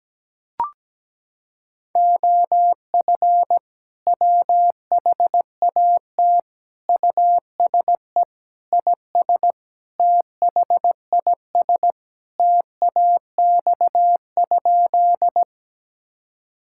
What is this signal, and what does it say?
Morse code, heard as a keyed tone, about 17 wpm, 700 Hz. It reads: OF WHAT USE IS THIS TAX?
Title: Sentences from Top 1000 Words - Review Part 5 - 17wpm